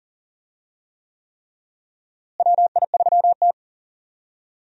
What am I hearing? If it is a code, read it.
Morse code, heard as a keyed tone, about 40 wpm, 700 Hz. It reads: WI3T
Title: USA Call Signs - 4 Characters - Repeated 3x - 40wpm